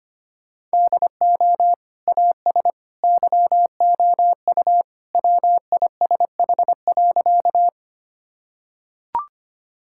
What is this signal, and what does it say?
Morse code, heard as a keyed tone, about 25 wpm, 700 Hz. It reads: DO AS YOU WISH.
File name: practice / 25wpm / Sentences with 2-Word Phrases - 25wpm